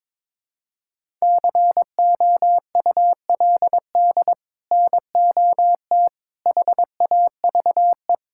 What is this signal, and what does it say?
Morse code, heard as a keyed tone, about 22 wpm, 700 Hz. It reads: COULD NOT HAVE
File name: practice / 22wpm / Sets of 8 Words - 22wpm